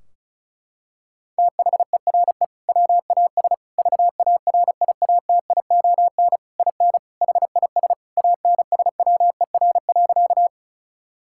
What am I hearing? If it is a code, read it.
Morse code, heard as a keyed tone, about 36 wpm, 700 Hz. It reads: THERE WAS VARIATION IN HIS ANSWER.
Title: Sentences from Top 1600 Words 35wpm